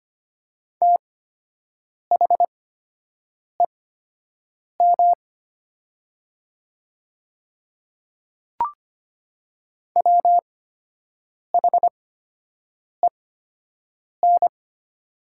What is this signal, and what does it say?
Morse code, heard as a keyed tone, about 25 wpm, spaced as for 7 wpm, 700 Hz. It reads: THEM WHEN
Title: 8x Character Spacing - Top 100 Words - 25wpm